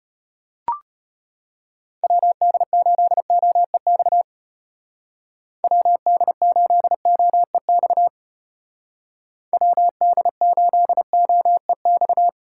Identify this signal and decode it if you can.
Morse code, heard as a keyed tone, about 30 wpm, 700 Hz. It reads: WD8OEX WD8OEX WD8OEX
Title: Speed Racing - USA Call Signs 25wpm